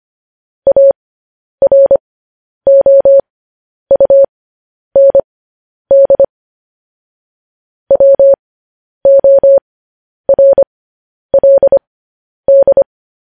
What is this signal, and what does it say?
Morse code, heard as a keyed tone, about 25 wpm, spaced as for 10 wpm, 550 Hz. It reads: AROUND WORLD
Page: news headlines in morse code at 10 wpm